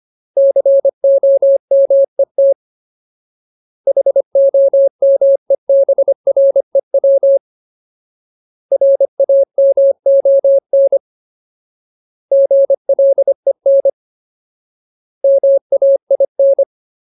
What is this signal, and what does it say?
Morse code, heard as a keyed tone, about 25 wpm, 550 Hz. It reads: COMET HOMEBREW RAMON GLEN MAIN
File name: Ultimate Word List / 25WPM Ultimate